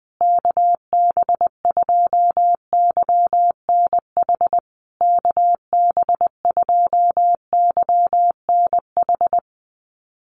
Morse code, heard as a keyed tone, about 20 wpm, 700 Hz. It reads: KB2YNH KB2YNH
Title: USA Call Signs Repeated 3x 20wpm